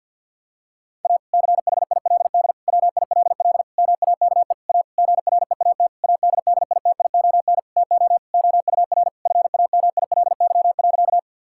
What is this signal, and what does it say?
Morse code, heard as a keyed tone, about 50 wpm, 700 Hz. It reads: A CHILD WILL MAKE A GREAT ADDITION TO OUR FAMILY.
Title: Sentences from Top 1000 Words - Review Part 6 - 50wpm